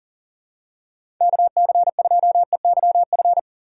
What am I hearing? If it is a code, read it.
Morse code, heard as a keyed tone, about 39 wpm, 700 Hz. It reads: KC2EYF
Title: USA Call Signs - 6 Characters - 6x Character Spacing - 40wpm